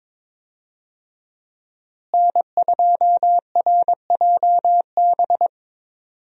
Morse code, heard as a keyed tone, about 22 wpm, 700 Hz. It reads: N2RJB
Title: USA Call Signs - 5 Characters - Repeated 2x - 22wpm